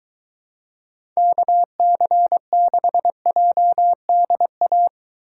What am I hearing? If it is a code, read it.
Morse code, heard as a keyed tone, about 21 wpm, 700 Hz. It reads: KC6JDA